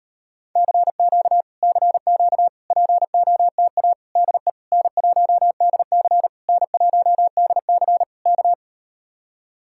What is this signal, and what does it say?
Morse code, heard as a keyed tone, about 38 wpm, 700 Hz. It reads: CQ CQ POTA DE N1DC N1DC K